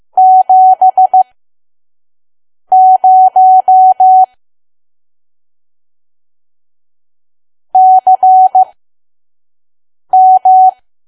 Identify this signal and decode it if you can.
Morse code, heard as a keyed tone, about 15 wpm, spaced as for 5 wpm, 750 Hz. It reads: 70 CM
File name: Archive / 5 WPM files / PM